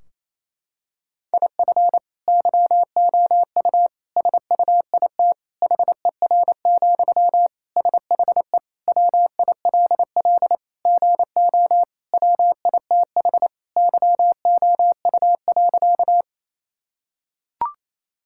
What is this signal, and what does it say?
Morse code, heard as a keyed tone, about 28 wpm, 700 Hz. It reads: IF YOU SUIT HER, SHE WILL GO WITH YOU.
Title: Sentences from Top 1600 Words 28wpm